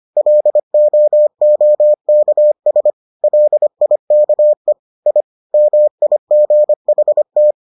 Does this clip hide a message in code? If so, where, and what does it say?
Yes: Morse code from 0.2 to 7.5 s, heard as a keyed tone, about 25 wpm, 600 Hz. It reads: LOOKS LIKE I MIGHT